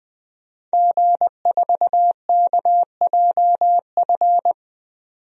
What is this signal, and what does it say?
Morse code, heard as a keyed tone, about 20 wpm, 700 Hz. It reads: G4KJF
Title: DXCC-100 Call Signs - 5 Characters - 20wpm